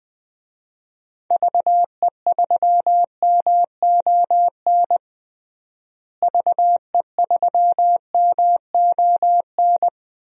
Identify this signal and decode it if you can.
Morse code, heard as a keyed tone, about 21 wpm, 700 Hz. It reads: VE3MON VE3MON
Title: DXCC-100 Call Signs - 6 Characters - Repeated 3x - 20wpm